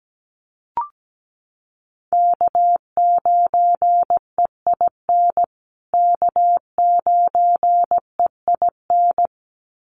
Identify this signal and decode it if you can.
Morse code, heard as a keyed tone, about 17 wpm, 700 Hz. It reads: K9EIN K9EIN
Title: USA Call Signs Repeated 2x 17wpm